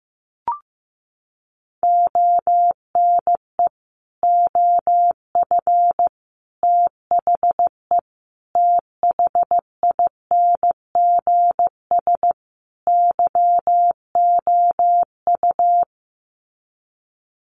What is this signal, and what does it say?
Morse code, heard as a keyed tone, about 15 wpm, 700 Hz. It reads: ONE OF THE THINGS YOU
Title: Sets of 5 Words Encore2 15wpm